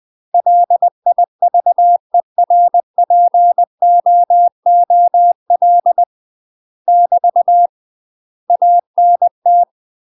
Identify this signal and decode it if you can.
Morse code, heard as a keyed tone, about 20 wpm, 700 Hz. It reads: LIVERPOOL <BT> ANT
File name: UK MASTER QSOs TRAINING/20WPM